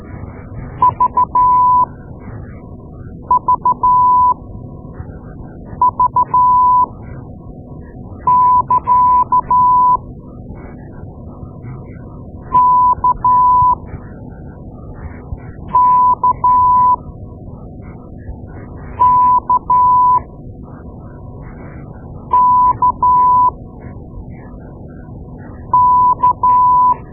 Morse code, heard as a keyed tone, about 13 wpm, spaced as for 4 wpm, 1000 Hz. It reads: VVV<KA>KKKKK
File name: Morse Lessons / Morse Lessons / Tape-2A